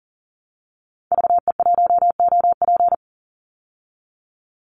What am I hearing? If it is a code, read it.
Morse code, heard as a keyed tone, about 39 wpm, 700 Hz. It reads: VE1OP
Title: Call Signs Repeated 3x - 40wpm